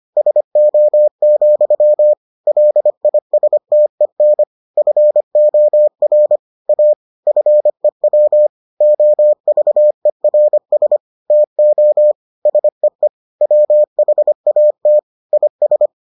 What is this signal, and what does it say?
Morse code, heard as a keyed tone, about 25 wpm, 600 Hz. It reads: SO, LISTEN FOR A FEW OVERS TO SEE WHAT IS